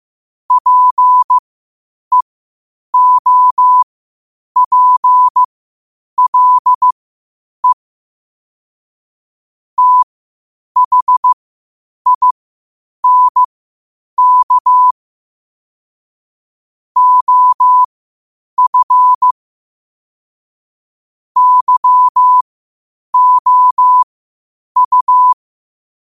Morse code, heard as a keyed tone, about 15 wpm, spaced as for 8 wpm, 1000 Hz. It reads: PEOPLE THINK OF YOU
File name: qotdfiles / 8wpm